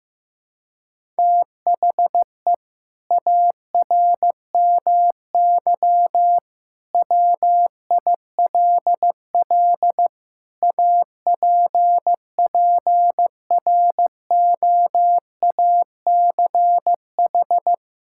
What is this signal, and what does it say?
Morse code, heard as a keyed tone, about 15 wpm, 700 Hz. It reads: THE ARMY WILL APPROACH
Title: Sentences from Top 1000 Words - Review Part 6 - 15wpm